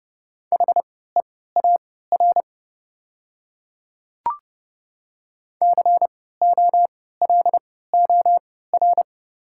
Morse code, heard as a keyed tone, about 32 wpm, spaced as for 17 wpm, 700 Hz. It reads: HEAR COLOR
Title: 3x Character Spacing - Top 300 Words - 30wpm